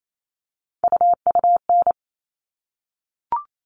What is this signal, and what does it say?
Morse code, heard as a keyed tone, about 28 wpm, 700 Hz. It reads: UUN
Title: All Cut Numbers - 3-Digit 28wpm